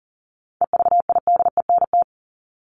Morse code, heard as a keyed tone, about 40 wpm, 700 Hz. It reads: EVIDENT
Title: LY Suffix - Top 100 Words with Base Words 40wpm